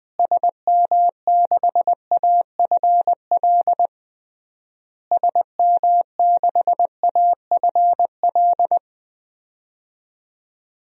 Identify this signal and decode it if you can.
Morse code, heard as a keyed tone, about 20 wpm, 700 Hz. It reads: SM6AFL SM6AFL
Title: DXCC-100 Call Signs - 6 Characters - Repeated 3x - 20wpm